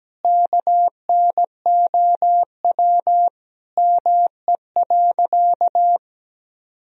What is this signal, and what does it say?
Morse code, heard as a keyed tone, about 17 wpm, 700 Hz. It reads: KNOW ME.
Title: Sentences from Top 200 Words and Abbreviations 17wpm